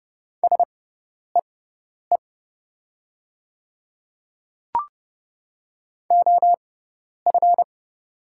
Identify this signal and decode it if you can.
Morse code, heard as a keyed tone, about 29 wpm, spaced as for 10 wpm, 700 Hz. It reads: SEE OF
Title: 6x Character Spacing - Top 100 Words - 30wpm